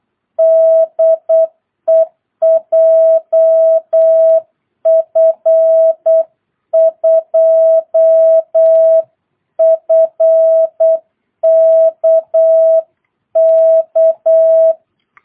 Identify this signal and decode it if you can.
Morse code, heard as a keyed tone, about 8 wpm, 660 Hz. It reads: DE JF2FKK
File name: jh8cbh / 2022cw